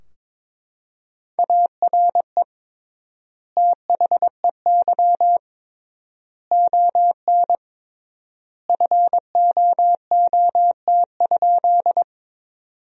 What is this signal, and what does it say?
Morse code, heard as a keyed tone, about 22 wpm, 700 Hz. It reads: ARE THEY ON FOOT?
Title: 3x Word-Spacing - Sentences from Top 500 Words - 22wpm